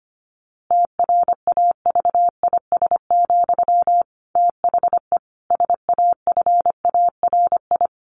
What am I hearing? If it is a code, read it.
Morse code, heard as a keyed tone, about 25 wpm, 700 Hz. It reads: TRAVIS, THE SAFARI
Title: A Sound Of Thunder 25wpm